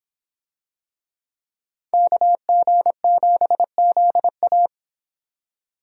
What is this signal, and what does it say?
Morse code, heard as a keyed tone, about 26 wpm, 700 Hz. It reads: KG7ZA